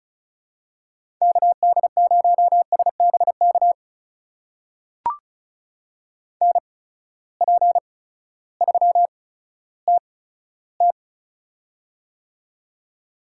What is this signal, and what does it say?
Morse code, heard as a keyed tone, about 36 wpm, 700 Hz. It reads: K D 0 S B K N P 3 T T
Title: USA Call Signs - 8x Character Spacing - 35wpm